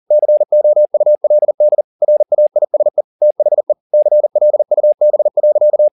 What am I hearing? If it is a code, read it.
Morse code, heard as a keyed tone, about 40 wpm, 600 Hz. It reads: COULD RAISE THE CLUB.
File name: the old man and the sea 2 40wpm